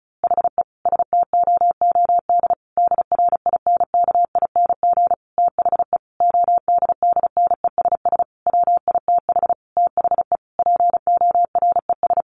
Morse code, heard as a keyed tone, about 34 wpm, 700 Hz. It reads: HE STOOD DRINKING THE ODDNESS WITH THE PORES
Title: A Sound Of Thunder 35wpm